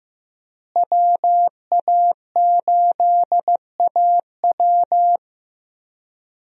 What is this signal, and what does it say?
Morse code, heard as a keyed tone, about 16 wpm, 700 Hz. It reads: WA8AW
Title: USA Call Signs - 5 Characters - Repeated 2x - 15wpm